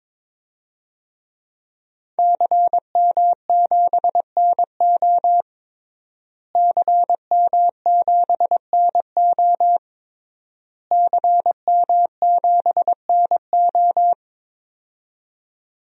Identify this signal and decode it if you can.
Morse code, heard as a keyed tone, about 22 wpm, 700 Hz. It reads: CM7NO CM7NO CM7NO
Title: DXCC-100 Call Signs - 5 Characters - Repeated 3x - 22wpm